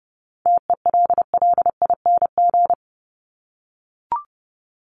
Morse code, heard as a keyed tone, about 30 wpm, 700 Hz. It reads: TELLING
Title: ING Suffix - Top 100 Words with Base Words 30wpm